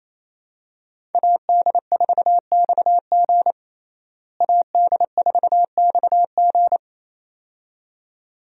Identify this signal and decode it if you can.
Morse code, heard as a keyed tone, about 28 wpm, 700 Hz. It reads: AD4XG AD4XG
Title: USA Call Signs - 5 Characters - Repeated 2x - 28wpm